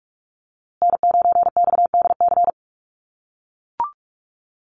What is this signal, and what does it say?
Morse code, heard as a keyed tone, about 44 wpm, 700 Hz. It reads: N9XDC
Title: USA Call Signs Repeated 2x 45wpm